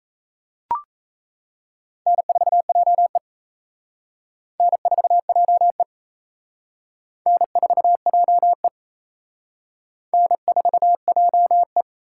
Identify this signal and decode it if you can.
Morse code, heard as a keyed tone, about 36 wpm, 700 Hz. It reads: N4JE N4JE N4JE N4JE